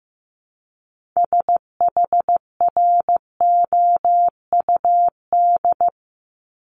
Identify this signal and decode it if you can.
Morse code, heard as a keyed tone, about 15 wpm, 700 Hz. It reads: SHROUD
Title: Multi-letter Phonemes 15wpm